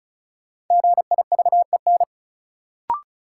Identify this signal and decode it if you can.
Morse code, heard as a keyed tone, about 36 wpm, 700 Hz. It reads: GIVEN
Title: Sentences from Top 1400 Words - New Word Drill - 35wpm